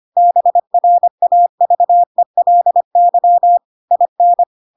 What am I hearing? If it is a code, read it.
Morse code, heard as a keyed tone, about 25 wpm, 700 Hz. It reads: BRAVELY IN
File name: Pinocchio / Pinocchio ch3 25 25 ik1hge